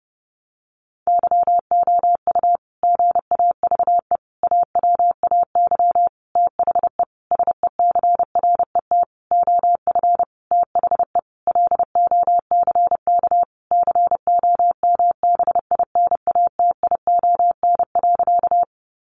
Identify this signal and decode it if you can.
Morse code, heard as a keyed tone, about 30 wpm, 700 Hz. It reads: YOU GAVE AWAY THE SECRET OF THE LOCK COMBINATION.
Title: Sentences from Top 1300 Words 30wpm